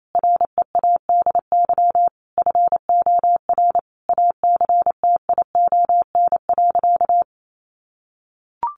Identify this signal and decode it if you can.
Morse code, heard as a keyed tone, about 28 wpm, 700 Hz. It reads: READY FOR ACTION.
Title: Sentences from Top 1300 Words 28wpm